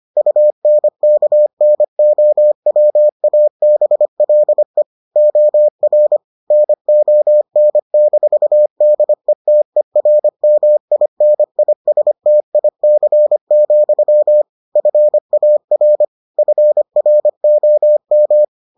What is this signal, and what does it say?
Morse code, heard as a keyed tone, about 25 wpm, 600 Hz. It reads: UNKNOWABLE OR NON-DETERMINISTIC, FAR FROM